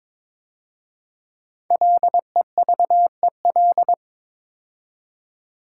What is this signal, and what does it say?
Morse code, heard as a keyed tone, about 22 wpm, 690 Hz. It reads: LEVEL
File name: 5 Letter Words - First 3 Letters Given - 22wpm